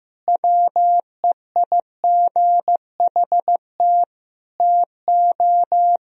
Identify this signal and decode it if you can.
Morse code, heard as a keyed tone, about 15 wpm, 700 Hz. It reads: WEIGHT TO
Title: Sentences from Top 1000 Words - Review Part 3 - 15wpm